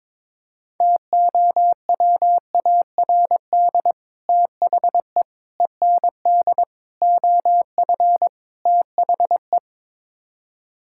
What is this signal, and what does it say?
Morse code, heard as a keyed tone, about 22 wpm, 700 Hz. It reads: TOWARD THE END OF THE
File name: practice / 22wpm / Build A Phrase - 5-Word Phrases - 22wpm